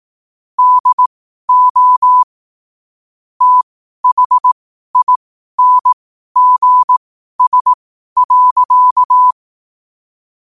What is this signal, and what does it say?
Morse code, heard as a keyed tone, about 18 wpm, spaced as for 12 wpm, 1000 Hz. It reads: DO THINGS.
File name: qotdfiles / 12wpm